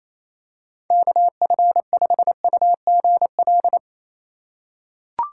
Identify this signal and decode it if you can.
Morse code, heard as a keyed tone, about 28 wpm, 700 Hz. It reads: KF5UGL
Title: USA Call Signs - 6 Characters - 2x Character Spacing - 28wpm